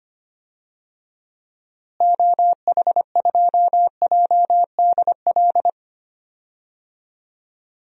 Morse code, heard as a keyed tone, about 25 wpm, 700 Hz. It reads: OH2JDL
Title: DXCC-100 Call Signs - 6 Characters - Repeated 2x - 25wpm